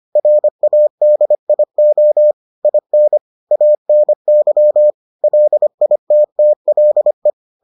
Morse code, heard as a keyed tone, about 25 wpm, 600 Hz. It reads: RADIO IN ANY LITTLE